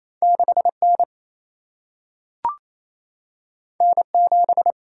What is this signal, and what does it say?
Morse code, heard as a keyed tone, about 28 wpm, 700 Hz. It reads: 6 N N 7
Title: Single-Letter Single-Number 28wpm